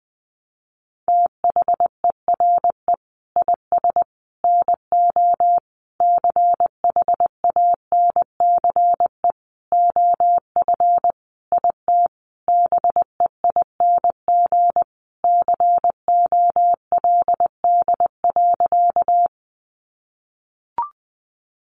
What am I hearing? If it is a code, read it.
Morse code, heard as a keyed tone, about 20 wpm, 700 Hz. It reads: THERE IS NO CHANCE OF IT BEING COLD.
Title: Sentences from Top 800 Words 20wpm